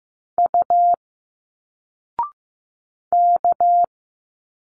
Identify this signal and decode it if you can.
Morse code, heard as a keyed tone, about 15 wpm, 700 Hz. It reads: U K
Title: Single Letters 15wpm